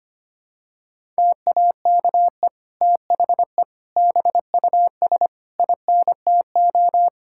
Morse code, heard as a keyed tone, about 25 wpm, 700 Hz. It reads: TAKE THE BUS INTO
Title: Sentences from Top 1000 Words - Review Part 6 - 25wpm